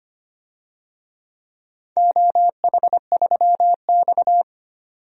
Morse code, heard as a keyed tone, about 25 wpm, 700 Hz. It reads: OH3X
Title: DXCC-100 Call Signs - 4 Characters - 25wpm